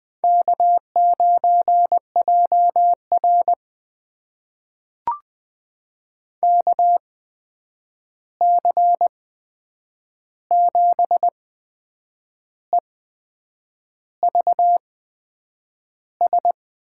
Morse code, heard as a keyed tone, about 20 wpm, spaced as for 8 wpm, 700 Hz. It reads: K9JR KC7EVS
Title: USA Call Signs - 8x Character Spacing - 20wpm